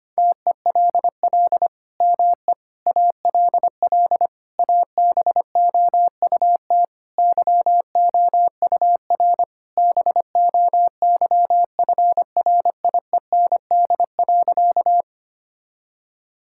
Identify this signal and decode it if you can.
Morse code, heard as a keyed tone, about 25 wpm, 700 Hz. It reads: TELL ME ALL ABOUT YOUR BOYFRIEND.
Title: Sentences from Top 1000 Words - Review Part 5 - 25wpm